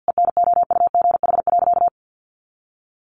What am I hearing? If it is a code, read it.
Morse code, heard as a keyed tone, about 50 wpm, 700 Hz. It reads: ENOUGH.
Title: Sentences from Top 400 Words 50wpm